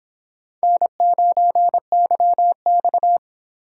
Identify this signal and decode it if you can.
Morse code, heard as a keyed tone, about 26 wpm, 700 Hz. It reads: N9YX